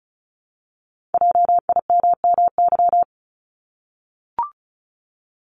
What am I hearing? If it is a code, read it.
Morse code, heard as a keyed tone, about 36 wpm, 700 Hz. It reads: JIMMY
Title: Multi-letter Phonemes Repeated 3x 35wpm